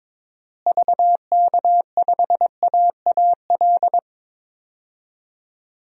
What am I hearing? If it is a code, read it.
Morse code, heard as a keyed tone, about 22 wpm, 700 Hz. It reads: VK5AAL
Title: DXCC-100 Call Signs - 6 Characters - Repeated 2x - 22wpm